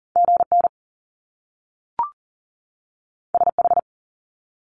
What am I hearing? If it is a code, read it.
Morse code, heard as a keyed tone, about 40 wpm, 700 Hz. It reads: GN SH
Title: Top Two Letter Combinations in Words 40wpm